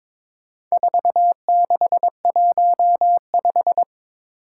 Morse code, heard as a keyed tone, about 22 wpm, 700 Hz. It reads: 4615
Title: Four Challenging Letter-Number Combinations 22wpm